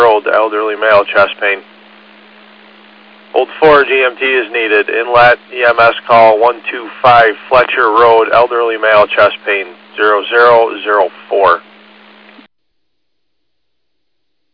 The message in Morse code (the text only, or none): none